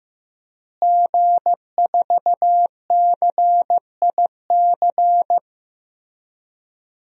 Morse code, heard as a keyed tone, about 15 wpm, 700 Hz. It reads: G4CIC
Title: DXCC-100 Call Signs - 5 Characters - Repeated 3x - 15wpm